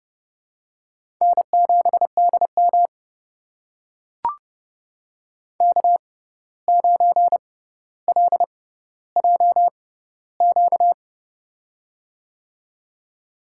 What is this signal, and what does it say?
Morse code, heard as a keyed tone, about 30 wpm, spaced as for 14 wpm, 700 Hz. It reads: N7DM K9LJQ
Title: USA Call Signs - 6x Character Spacing - 30wpm